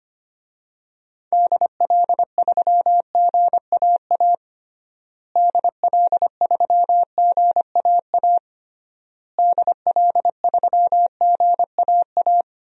Morse code, heard as a keyed tone, about 25 wpm, 700 Hz. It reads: DL3GAA DL3GAA DL3GAA
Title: DXCC-100 Call Signs - 6 Characters - Repeated 3x - 25wpm